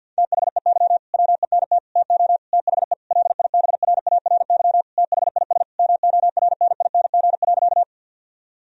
Morse code, heard as a keyed tone, about 50 wpm, 700 Hz. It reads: THEY WENT TO THE LIBRARY THIS MORNING.
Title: Sentences from Top 1000 Words - Review Part 4 - 50wpm